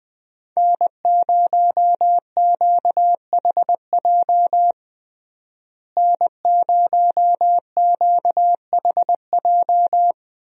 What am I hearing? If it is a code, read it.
Morse code, heard as a keyed tone, about 20 wpm, 700 Hz. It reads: N0QHJ N0QHJ
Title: USA Call Signs - 5 Characters - Repeated 2x - 20wpm